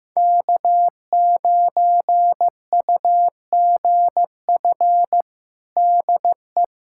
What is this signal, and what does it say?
Morse code, heard as a keyed tone, about 15 wpm, 700 Hz. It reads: K9UGF DE
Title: USA Callsign DE Callsign - 5 Characters - 15wpm